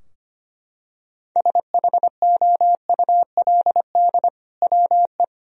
Morse code, heard as a keyed tone, about 25 wpm, 700 Hz. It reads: SHOULD WE
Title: Sentences from Top 1000 Words - Review Part 3 - 25wpm